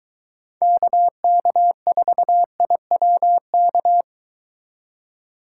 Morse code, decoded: KK4IWK

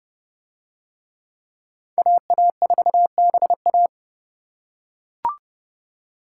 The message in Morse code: AA4BA

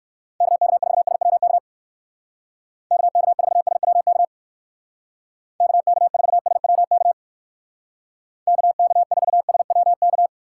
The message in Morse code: KK4SWK KK4SWK KK4SWK KK4SWK

